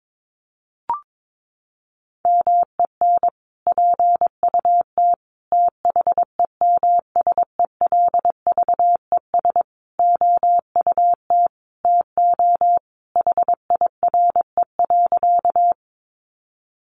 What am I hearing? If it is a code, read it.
Morse code, heard as a keyed tone, about 22 wpm, 700 Hz. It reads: MEN PUT THEMSELVES OUT TO HIRE.
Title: Sentences from Top 1400 Words 22wpm